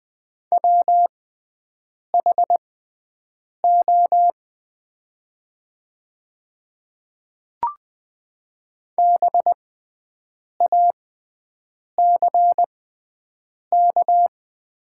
Morse code, heard as a keyed tone, about 20 wpm, spaced as for 7 wpm, 700 Hz. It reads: WHO BACK
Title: 6x Character Spacing - Top 300 Words - 20wpm